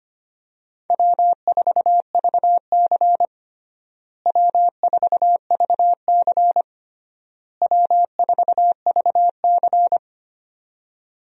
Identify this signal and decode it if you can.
Morse code, heard as a keyed tone, about 25 wpm, 700 Hz. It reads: W4VC W4VC W4VC
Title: USA Call Signs - 4 Characters - Repeated 3x - 25wpm